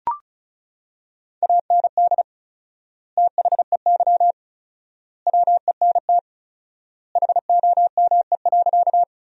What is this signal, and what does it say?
Morse code, heard as a keyed tone, about 34 wpm, 700 Hz. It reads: AND THEY WENT HOME.
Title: 4x Word-Spacing - Sentences from Top 200 Words - 35wpm